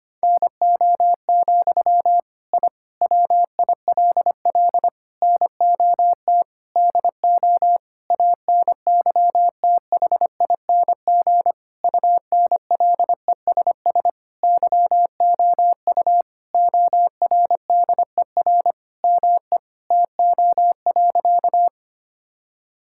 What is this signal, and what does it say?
Morse code, heard as a keyed tone, about 25 wpm, 700 Hz. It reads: NO, I WILL NOT DO ANYTHING UNLESS YOU ORDER ME TO.
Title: Sentences from Top 1000 Words - Review Part 6 - 25wpm